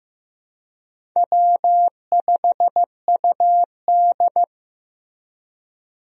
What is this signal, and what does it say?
Morse code, heard as a keyed tone, about 15 wpm, 700 Hz. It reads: W5UD
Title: USA Call Signs - 4 Characters - Repeated 3x - 15wpm